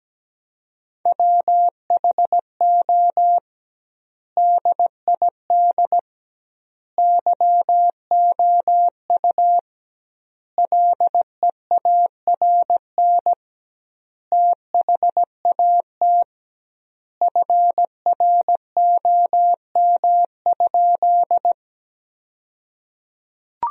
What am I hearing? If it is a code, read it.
Morse code, heard as a keyed tone, about 17 wpm, 700 Hz. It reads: WHO DID YOU LEARN THAT FROM?
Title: 2x Word-Spacing - Sentences from Top 300 Words - 17wpm